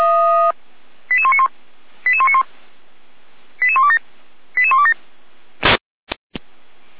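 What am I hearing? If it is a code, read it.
Five-tone selective call (ZVEI), heard as quick runs of five tones, tones of about 70 ms. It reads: 88191 88191 88127 88127
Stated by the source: ZVEI five-tone selective call 88119 received on SDR.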